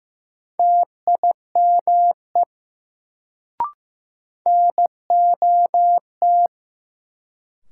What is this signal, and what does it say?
Morse code, heard as a keyed tone, about 15 wpm, 700 Hz. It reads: TIME NOT
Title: Sets of 2 Words - New Word Drill - 15wpm